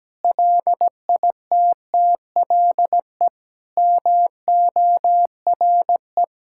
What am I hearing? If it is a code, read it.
Morse code, heard as a keyed tone, about 17 wpm, 700 Hz. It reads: LITTLE MORE
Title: Sentences from Top 1600 Words 17wpm